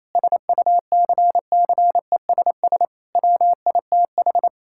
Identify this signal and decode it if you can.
Morse code, heard as a keyed tone, about 27 wpm, 700 Hz. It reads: SUCCESS WITH
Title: Sentences from Top 1000 Words - Review Part 6 - 28wpm